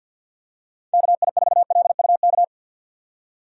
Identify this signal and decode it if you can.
Morse code, heard as a keyed tone, about 52 wpm, 700 Hz. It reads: KI4LUK
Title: USA Call Signs - 6 Characters - 4x Character Spacing - 50wpm